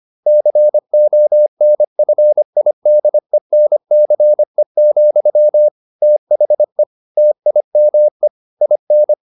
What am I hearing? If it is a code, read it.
Morse code, heard as a keyed tone, about 25 wpm, 600 Hz. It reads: CONFIDENCE, THE TIME IN
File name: the old man and the sea part 1 25wpm